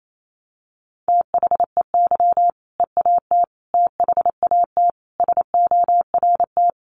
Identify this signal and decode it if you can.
Morse code, heard as a keyed tone, about 28 wpm, 700 Hz. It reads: THEY EAT THAT SORT
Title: Sentences from Top 1400 Words 28wpm